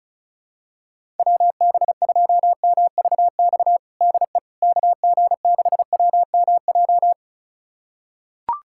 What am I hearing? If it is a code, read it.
Morse code, heard as a keyed tone, about 34 wpm, 700 Hz. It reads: WB2MVX DE KG6WMJ
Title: USA Callsign DE Callsign - 6 Characters - 35wpm